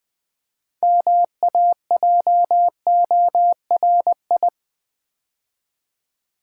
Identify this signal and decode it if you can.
Morse code, heard as a keyed tone, about 21 wpm, 700 Hz. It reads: MAJORI
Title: Build A Word - 2 Letters At A Time - 8-Letter Words - 20wpm